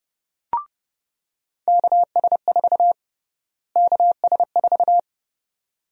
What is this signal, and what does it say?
Morse code, heard as a keyed tone, about 30 wpm, 700 Hz. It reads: KS4 KS4
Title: USA Call Sign Prefixes - Repeated 2x - 30wpm